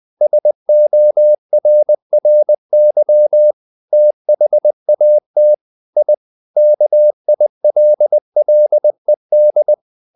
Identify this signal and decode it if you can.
Morse code, heard as a keyed tone, about 20 wpm, 600 Hz. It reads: SORRY THAT I KILLED
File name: the old man and the sea part 2 20wpm